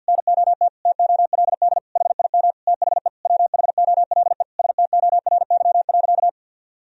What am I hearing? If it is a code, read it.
Morse code, heard as a keyed tone, about 51 wpm, 700 Hz. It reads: NOT TOLD HIM THE WHOLE STORY.
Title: Sentences from Top 1000 Words - Review Part 1 - 50wpm